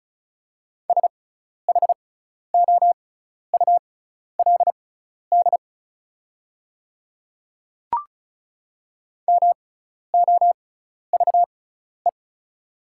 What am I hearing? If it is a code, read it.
Morse code, heard as a keyed tone, about 36 wpm, spaced as for 12 wpm, 700 Hz. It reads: SHOULD MOVE
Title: 6x Character Spacing - Top 300 Words - 35wpm